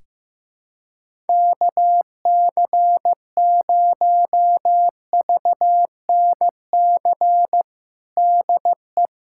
Morse code, heard as a keyed tone, about 15 wpm, 700 Hz. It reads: KC0VNC DE
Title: USA Callsign DE Callsign 15wpm